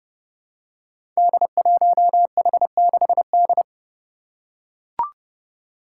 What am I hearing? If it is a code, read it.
Morse code, heard as a keyed tone, about 30 wpm, 700 Hz. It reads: D1H6D
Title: Five Challenging Letter-Number Combinations 30wpm